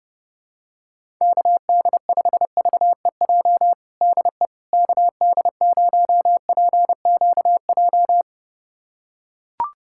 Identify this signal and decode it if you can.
Morse code, heard as a keyed tone, about 30 wpm, 700 Hz. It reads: KD5VEJ DE KD0PQJ
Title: USA Callsign DE Callsign - 6 Characters - 30wpm